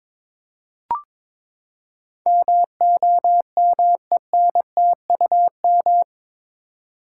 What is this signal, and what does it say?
Morse code, heard as a keyed tone, about 22 wpm, 700 Hz. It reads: MOMENTUM
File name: practice / 22wpm / Eight Letter Words - 22wpm